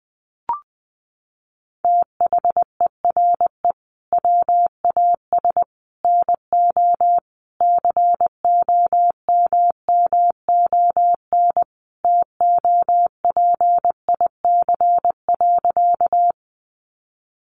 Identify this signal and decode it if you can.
Morse code, heard as a keyed tone, about 20 wpm, 700 Hz. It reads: THERE WAS NO COMMON TOPIC.
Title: Sentences from Top 1200 Words 20wpm